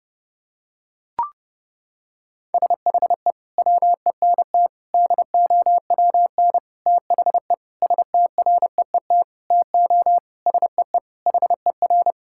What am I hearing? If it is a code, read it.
Morse code, heard as a keyed tone, about 30 wpm, 700 Hz. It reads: SHE WENT DOWN THE STREET TO SEE HER
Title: Sentences from Top 1000 Words - Review Part 5 - 30wpm